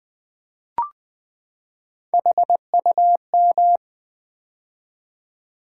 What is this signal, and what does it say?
Morse code, heard as a keyed tone, about 20 wpm, 690 Hz. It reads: HUM
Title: Build A Word - 3 Letters At A Time - 6-Letter Words - 20wpm